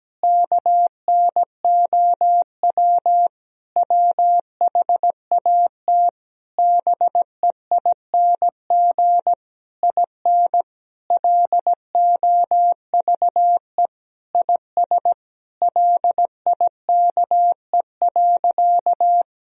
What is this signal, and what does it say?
Morse code, heard as a keyed tone, about 17 wpm, 700 Hz. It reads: KNOW WHAT BEING IN LOVE IS LIKE.
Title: Sentences from Top 1000 Words - Review Part 3 - 17wpm